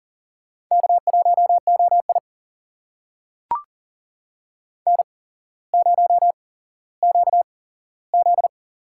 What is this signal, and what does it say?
Morse code, heard as a keyed tone, about 38 wpm, spaced as for 17 wpm, 700 Hz. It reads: K1OI N0QZ
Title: USA Call Signs - 4 Characters - 8x Character Spacing - 40wpm